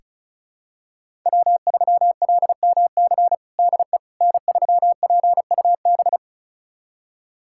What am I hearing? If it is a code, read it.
Morse code, heard as a keyed tone, about 36 wpm, 700 Hz. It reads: W3LMC DE N3PUB
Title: USA Callsign DE Callsign - 5 Characters - 35wpm